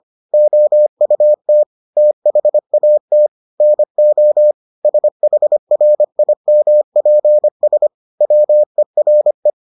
Morse code, heard as a keyed tone, about 25 wpm, 600 Hz. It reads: OUT THAT NO SHRIMPS WERE